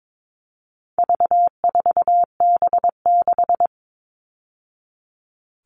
Morse code, heard as a keyed tone, about 22 wpm, 700 Hz. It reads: V4B6
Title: All Cut Numbers - 4-Digit 22wpm